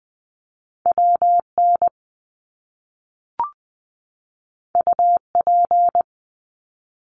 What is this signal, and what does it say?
Morse code, heard as a keyed tone, about 20 wpm, 700 Hz. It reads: WN UP
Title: Top Two Letter Combinations in Words 20wpm